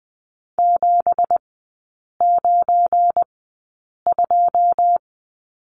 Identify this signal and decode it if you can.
Morse code, heard as a keyed tone, about 20 wpm, spaced as for 8 wpm, 690 Hz. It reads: 792
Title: 191 Lesson 31e - Letter v - Three Characters 20wpm